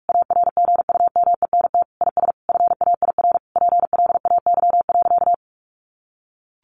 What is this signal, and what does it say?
Morse code, heard as a keyed tone, about 44 wpm, 700 Hz. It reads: ARGUMENT IS FAIR PLAY.
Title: Sentences from Top 1400 Words 45wpm